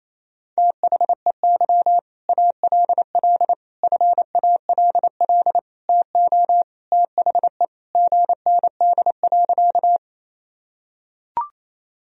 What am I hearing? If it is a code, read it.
Morse code, heard as a keyed tone, about 28 wpm, 700 Hz. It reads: THEY ALL FALL TO THE GND.
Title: Sentences from Top 500 Words and Abbreviations 28wpm